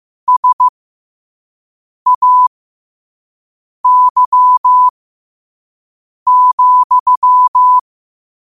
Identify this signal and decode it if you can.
Morse code, heard as a keyed tone, about 15 wpm, spaced as for 5 wpm, 1000 Hz. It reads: SAY,
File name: qotdfiles / 5wpm